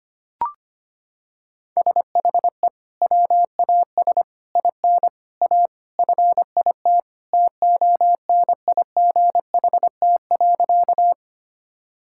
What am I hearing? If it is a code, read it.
Morse code, heard as a keyed tone, about 25 wpm, 700 Hz. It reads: SHE WAS IN A FIT TONIGHT.
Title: Sentences from Top 1000 Words - Review Part 5 - 25wpm